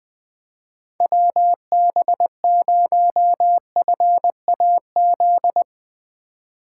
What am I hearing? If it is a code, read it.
Morse code, heard as a keyed tone, about 20 wpm, 700 Hz. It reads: WB0FAZ
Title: USA Call Signs - 6 Characters - 3x Character Spacing - 20wpm